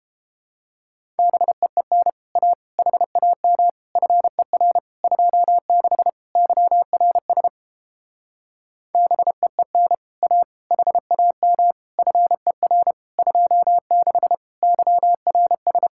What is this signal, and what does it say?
Morse code, heard as a keyed tone, about 31 wpm, 700 Hz. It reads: BEEN A HAM FER 26 YRS BEEN A HAM FER 26 YRS